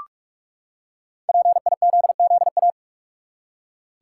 Morse code, heard as a keyed tone, about 46 wpm, 700 Hz. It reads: PIZZA
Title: Five Letter Words - 45wpm